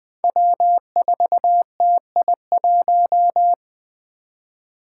Morse code, heard as a keyed tone, about 20 wpm, 700 Hz. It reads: W4TI1
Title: Five Challenging Letter-Number Combinations 20wpm